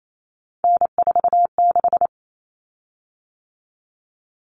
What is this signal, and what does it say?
Morse code, heard as a keyed tone, about 28 wpm, 700 Hz. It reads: N46
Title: Common Cut Numbers - 3-Digit 28wpm